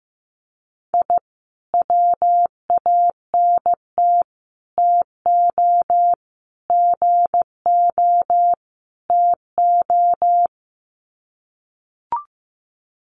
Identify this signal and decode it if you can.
Morse code, heard as a keyed tone, about 15 wpm, 700 Hz. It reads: I WANT TO GO TO
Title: Sets of 5 Words Encore 15wpm